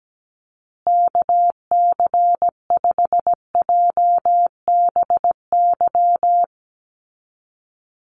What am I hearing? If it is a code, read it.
Morse code, heard as a keyed tone, about 17 wpm, 700 Hz. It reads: KC5JBY
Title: USA Call Signs Repeated 3x 17wpm